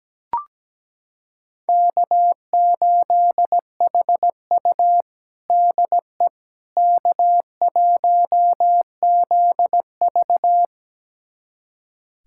Morse code, heard as a keyed tone, about 17 wpm, 700 Hz. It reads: K8HU DE K1ZV
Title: USA Callsign DE Callsign - 4 Characters - 17wpm